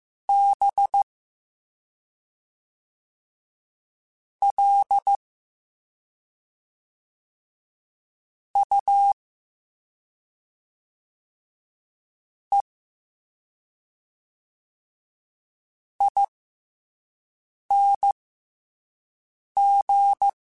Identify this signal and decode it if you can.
Morse code, heard as a keyed tone, about 15 wpm, spaced as for 3 wpm, 800 Hz. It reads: BLUEING